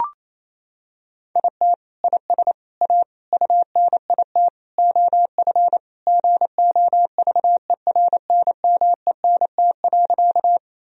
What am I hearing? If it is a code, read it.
Morse code, heard as a keyed tone, about 27 wpm, 700 Hz. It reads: IT IS A UNIT OF GOVERNMENT.
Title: Sentences from Top 1000 Words - Review Part 2 - 28wpm